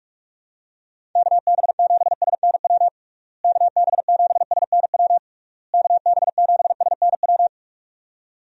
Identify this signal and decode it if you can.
Morse code, heard as a keyed tone, about 46 wpm, 700 Hz. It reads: KB7SNW KB7SNW KB7SNW
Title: USA Call Signs - 6 Characters - Repeated 3x - 45wpm